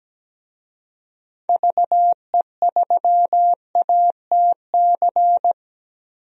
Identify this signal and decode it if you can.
Morse code, heard as a keyed tone, about 17 wpm, 700 Hz. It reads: VE3ATC
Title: DXCC-100 Call Signs - 6 Characters - 17wpm